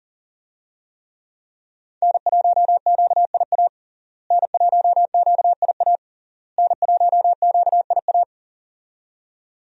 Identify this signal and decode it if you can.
Morse code, heard as a keyed tone, about 40 wpm, 700 Hz. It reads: N1QIA N1QIA N1QIA
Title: USA Call Signs - 5 Characters - Repeated 3x - 40wpm